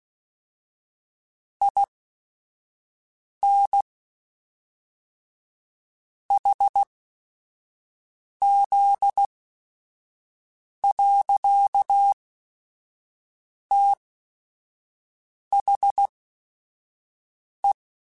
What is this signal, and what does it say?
Morse code, heard as a keyed tone, about 16 wpm, spaced as for 5 wpm, 800 Hz. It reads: IN HZ. THE
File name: Technical Subjects / Technical Subjects-mp3s / XC @5